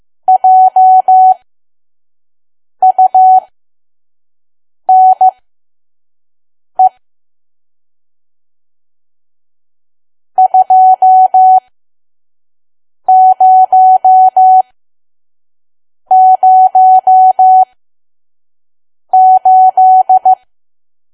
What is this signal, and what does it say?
Morse code, heard as a keyed tone, about 15 wpm, spaced as for 5 wpm, 750 Hz. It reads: JUNE 2008
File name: Archive / 5 WPM files / PM